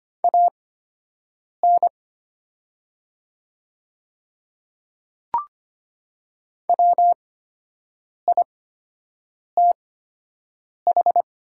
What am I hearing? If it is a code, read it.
Morse code, heard as a keyed tone, about 25 wpm, spaced as for 7 wpm, 700 Hz. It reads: AN WITH
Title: 8x Character Spacing - Top 100 Words - 25wpm